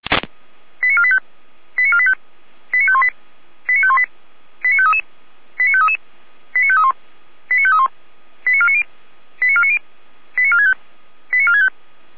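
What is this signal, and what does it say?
Five-tone selective call (ZVEI), heard as quick runs of five tones, tones of about 70 ms. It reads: 89485 89485 89518 89518 89533 89533 89531 89531 89490 89490 89465 89465
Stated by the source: Five-tone ZVEI selective call 89380, received by SDR.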